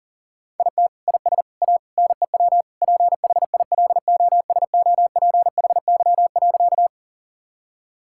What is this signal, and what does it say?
Morse code, heard as a keyed tone, about 40 wpm, 690 Hz. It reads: IT IS A NEW PHILOSOPHY.
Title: Sentences from Top 1000 Words - Review Part 6 - 40wpm